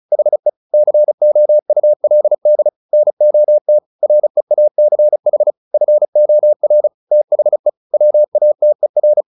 Morse code, heard as a keyed tone, about 34 wpm, 600 Hz. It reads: HE COULD NOT REACH FOR THE WATER
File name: the old man and the sea 2 35wpm